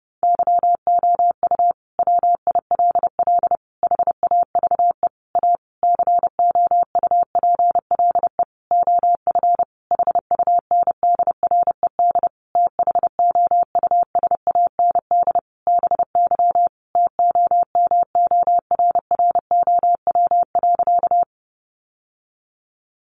Morse code, heard as a keyed tone, about 30 wpm, 700 Hz. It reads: YOU WILL HAVE A COUPLE OF HUNDRED THOUSAND BY TOMORROW.
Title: Sentences from Top 1100 Words 30wpm